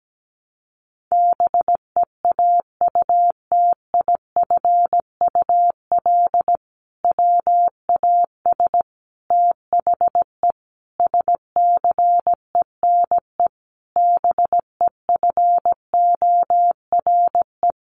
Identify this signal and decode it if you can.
Morse code, heard as a keyed tone, about 17 wpm, 700 Hz. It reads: BEAUTIFUL WAS THE SCENE BEFORE
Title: Sentences from Top 1100 Words 17wpm